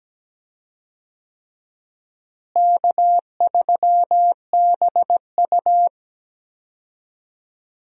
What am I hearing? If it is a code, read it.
Morse code, heard as a keyed tone, about 17 wpm, 700 Hz. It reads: K3BU
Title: USA Call Signs - 4 Characters - Repeated 3x - 17wpm